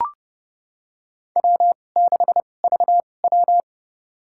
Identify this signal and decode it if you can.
Morse code, heard as a keyed tone, about 31 wpm, spaced as for 22 wpm, 700 Hz. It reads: W6VW
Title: USA Call Signs - 4 Characters - 2x Character Spacing - 30wpm